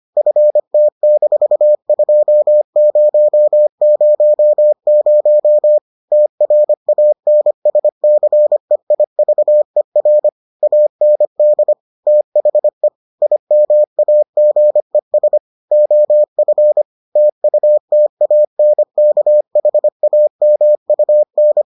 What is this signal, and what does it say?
Morse code, heard as a keyed tone, about 25 wpm, 600 Hz. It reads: FT-2000 TRANSCEIVER AND THE IMAGES OF TUTANKHAMUN